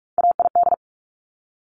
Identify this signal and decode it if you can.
Morse code, heard as a keyed tone, about 46 wpm, 700 Hz. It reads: AID
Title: Top Three Letter Combinations in Words 45wpm